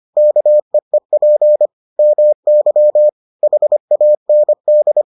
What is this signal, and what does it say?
Morse code, heard as a keyed tone, about 25 wpm, 600 Hz. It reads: KEEP MY HAND